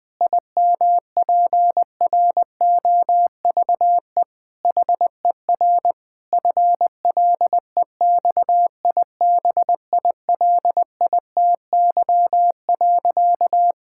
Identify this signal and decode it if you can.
Morse code, heard as a keyed tone, about 20 wpm, 700 Hz. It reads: IMPROVE HER FLEXIBILITY.
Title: Difficult Sentences with Hints - 20wpm